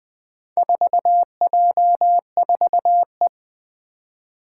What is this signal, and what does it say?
Morse code, heard as a keyed tone, about 21 wpm, 700 Hz. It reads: 4J4E
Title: Four Challenging Letter-Number Combinations 20wpm